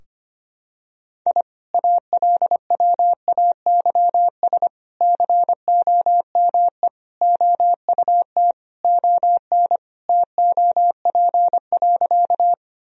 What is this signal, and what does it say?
Morse code, heard as a keyed tone, about 25 wpm, 700 Hz. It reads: I ALWAYS COME OUT ON TOP.
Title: Sentences from Top 1000 Words - Review Part 2 - 25wpm